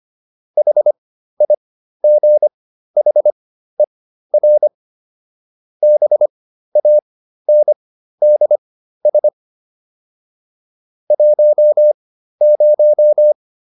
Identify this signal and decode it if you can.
Morse code, heard as a keyed tone, about 25 wpm, spaced as for 13 wpm, 600 Hz. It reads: HIGHER BANDS 10